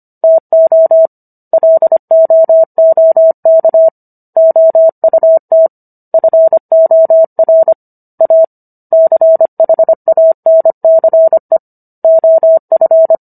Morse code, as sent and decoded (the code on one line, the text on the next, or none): - --- / .-.. --- --- -.- / --- ..- - / ..-. --- .-. / .- / -.-. .... .- -. -.-. / --- ..-.
TO LOOK OUT FOR A CHANCE OF